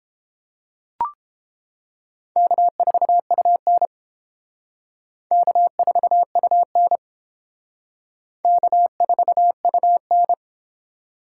Morse code, heard as a keyed tone, about 30 wpm, 700 Hz. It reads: K4UN K4UN K4UN